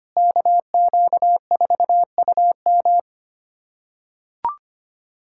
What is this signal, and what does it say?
Morse code, heard as a keyed tone, about 25 wpm, 700 Hz. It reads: KQ4UM
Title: USA Callsign DE Callsign - 5 Characters - 25wpm